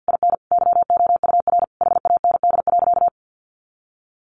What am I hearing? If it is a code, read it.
Morse code, heard as a keyed tone, about 50 wpm, 700 Hz. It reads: IN YOUR HAND.
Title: Sentences from Top 400 Words 50wpm